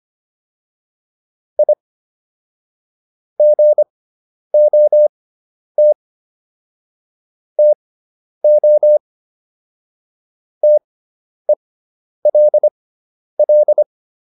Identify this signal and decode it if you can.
Morse code, heard as a keyed tone, about 25 wpm, spaced as for 10 wpm, 600 Hz. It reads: I GOT TO TELL
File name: WD401 10